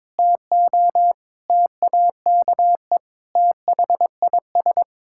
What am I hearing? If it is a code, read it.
Morse code, heard as a keyed tone, about 22 wpm, 700 Hz. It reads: TO TAKE THIS
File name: practice / 22wpm / Sets of 8 Words - 22wpm